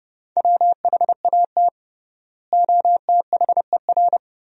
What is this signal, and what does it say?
Morse code, heard as a keyed tone, about 30 wpm, 700 Hz. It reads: WHAT OTHER
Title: 3x Word-Spacing - Sentences from Top 100 Words - 30wpm